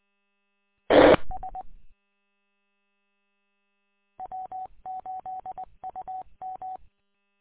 Morse code, heard as a keyed tone, about 20 wpm, 750 Hz. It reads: TS W8UM
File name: w8um- -combined